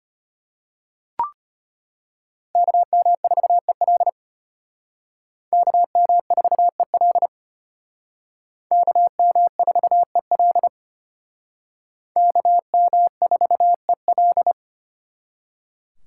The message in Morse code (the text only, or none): KM4EL KM4EL KM4EL KM4EL